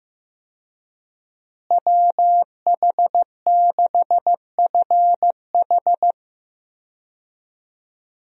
Morse code, heard as a keyed tone, about 15 wpm, 700 Hz. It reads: WH6FH
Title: USA Call Signs - 5 Characters - Repeated 3x - 15wpm